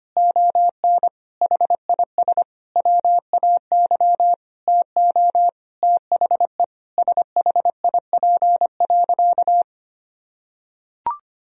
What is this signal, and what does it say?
Morse code, heard as a keyed tone, about 25 wpm, 700 Hz. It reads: ON HIS WAY TO THE SHIP.
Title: Sentences from Top 1000 Words - Review Part 2 - 25wpm